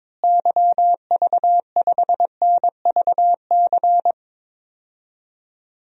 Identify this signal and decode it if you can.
Morse code, heard as a keyed tone, about 21 wpm, 700 Hz. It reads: YV5NVC
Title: DXCC-100 Call Signs - 6 Characters - Repeated 3x - 22wpm